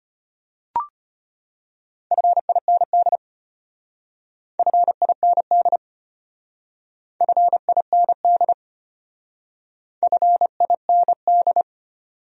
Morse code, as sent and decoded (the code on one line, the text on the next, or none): ..-. .. -. -.. / ..-. .. -. -.. / ..-. .. -. -.. / ..-. .. -. -..
FIND FIND FIND FIND